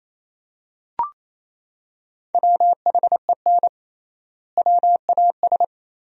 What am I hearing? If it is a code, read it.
Morse code, heard as a keyed tone, about 28 wpm, 700 Hz. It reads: WHEN WAS
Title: 3x Word-Spacing - Sentences from Top 100 Words - 28wpm